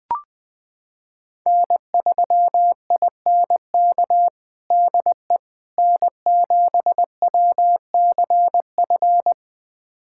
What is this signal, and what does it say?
Morse code, heard as a keyed tone, about 19 wpm, 700 Hz. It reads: N3INK DE N7WCF
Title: USA Callsign DE Callsign - 5 Characters - 20wpm